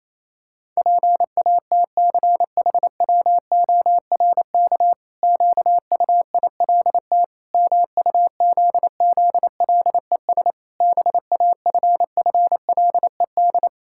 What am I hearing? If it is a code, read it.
Morse code, heard as a keyed tone, about 28 wpm, 700 Hz. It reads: PATCHWORK QUILT MUZZLES BAFFLED
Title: Pangram Challenge v2 - 28wpm